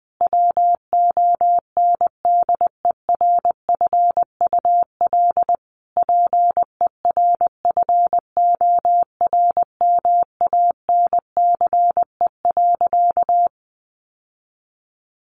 Morse code, heard as a keyed tone, about 20 wpm, 700 Hz. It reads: WONDERFUL PERFORMANCE.
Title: Sentences from Top 1100 Words 20wpm